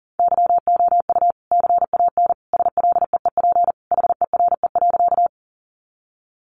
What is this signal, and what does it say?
Morse code, heard as a keyed tone, about 40 wpm, 700 Hz. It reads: YOU CAN SLEEP HERE.
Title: Sentences from Top 500 Words 40wpm